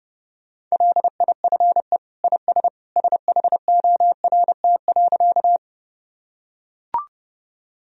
Morse code, heard as a keyed tone, about 32 wpm, 700 Hz. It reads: LIFE IS SHORT.
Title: Sentences from Top 1000 Words - Review Part 2 - 30wpm